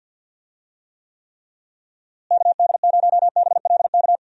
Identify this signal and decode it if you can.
Morse code, heard as a keyed tone, about 52 wpm, 700 Hz. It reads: KD0BLK
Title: USA Call Signs - 6 Characters - Repeated 3x - 50wpm